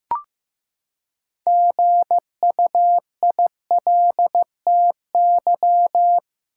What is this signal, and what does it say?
Morse code, heard as a keyed tone, about 15 wpm, 700 Hz. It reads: GUILTY
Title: Six Letter Words - 15wpm